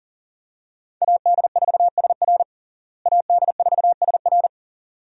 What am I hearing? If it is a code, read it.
Morse code, heard as a keyed tone, about 40 wpm, 700 Hz. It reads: AD4SR AD4SR
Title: USA Call Signs - 5 Characters - Repeated 2x - 40wpm